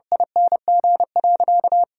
Morse code, heard as a keyed tone, about 32 wpm, 700 Hz. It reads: ING.
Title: 2x Word-Spacing - Sentences from Top 400 Words - 30wpm